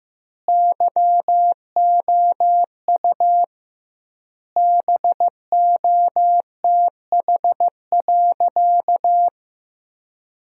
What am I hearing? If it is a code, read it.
Morse code, heard as a keyed tone, about 15 wpm, 700 Hz. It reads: YOU BOTH.
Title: 2x Word-Spacing - Sentences from Top 300 Words - 15wpm